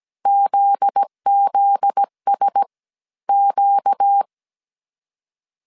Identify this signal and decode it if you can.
Morse code, heard as a keyed tone, about 17 wpm, 800 Hz. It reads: ZZS Q
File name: CW / DET / 1 9 1500C S A 16wpm 2828